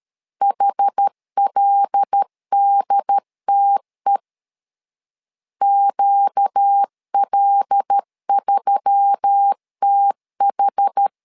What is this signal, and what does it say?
Morse code, heard as a keyed tone, about 13 wpm, 800 Hz. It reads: HLDTE QL3TH
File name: LEZIONI / 5 LBVC / 5 L 12wpm 1913